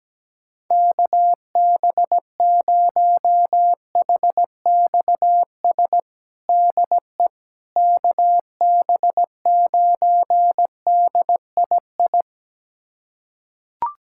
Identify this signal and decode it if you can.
Morse code, heard as a keyed tone, about 17 wpm, 700 Hz. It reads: KB0HXS DE KB9DII
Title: USA Callsign DE Callsign - 6 Characters - 17wpm